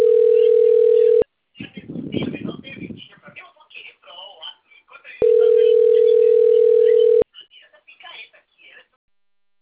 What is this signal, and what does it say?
Ringback tone (North America), 440 + 480 Hz, with speech.
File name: rg-1600-unknown- - - .42816